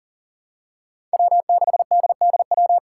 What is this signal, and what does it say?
Morse code, heard as a keyed tone, about 38 wpm, 700 Hz. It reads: W6DDW
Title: Five Challenging Letter-Number Combinations 40wpm